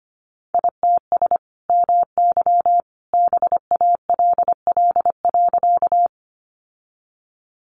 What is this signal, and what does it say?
Morse code, heard as a keyed tone, about 25 wpm, 700 Hz. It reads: ITS MY BALL.
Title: Sentences from Top 900 Words 25wpm